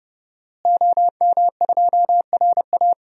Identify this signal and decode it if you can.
Morse code, heard as a keyed tone, about 30 wpm, 700 Hz. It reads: OM2RA
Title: DXCC-100 Call Signs - 5 Characters - Repeated 2x - 30wpm